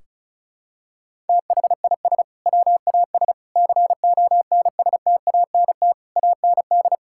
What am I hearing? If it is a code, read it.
Morse code, heard as a keyed tone, about 34 wpm, 700 Hz. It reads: THIS WAS CONSTANT AND
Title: Sentences from Top 1600 Words 35wpm